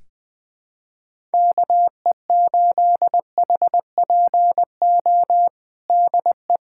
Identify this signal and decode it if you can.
Morse code, heard as a keyed tone, about 21 wpm, 700 Hz. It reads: KE8HPO DE
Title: USA Callsign DE Callsign - 6 Characters - 20wpm